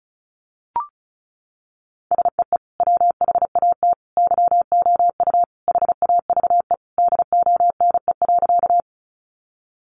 Morse code, heard as a keyed tone, about 34 wpm, 700 Hz. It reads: SEE WHAT YOU HAVE DONE.
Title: Sentences from Top 500 Words 35wpm